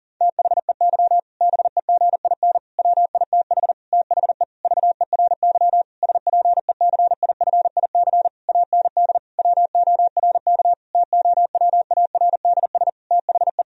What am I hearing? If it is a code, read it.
Morse code, heard as a keyed tone, about 40 wpm, 700 Hz. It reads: THEY BEGIN WITH THE VERY SPECIFIC AND WORK TOWARDS THE
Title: Sentences from Top 1000 Words - Review Part 3 - 40wpm